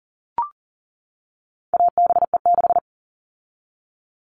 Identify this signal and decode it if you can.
Morse code, heard as a keyed tone, about 40 wpm, 700 Hz. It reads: ABE6
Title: All Cut Numbers - 4-Digit 40wpm